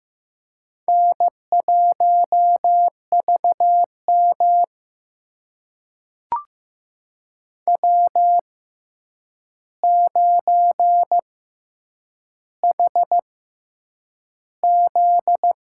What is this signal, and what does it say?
Morse code, heard as a keyed tone, about 15 wpm, spaced as for 8 wpm, 700 Hz. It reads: N1VM W9HZ
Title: USA Call Signs - 4 Characters - 6x Character Spacing - 15wpm